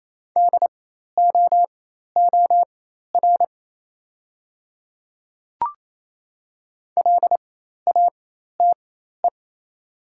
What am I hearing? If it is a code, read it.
Morse code, heard as a keyed tone, about 28 wpm, spaced as for 13 wpm, 700 Hz. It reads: DOOR LATE
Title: 4x Character Spacing - Top 300 Words - 28wpm